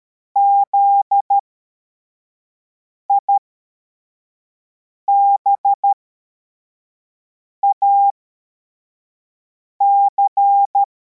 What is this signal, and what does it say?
Morse code, heard as a keyed tone, about 13 wpm, spaced as for 4 wpm, 800 Hz. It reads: ZIBAC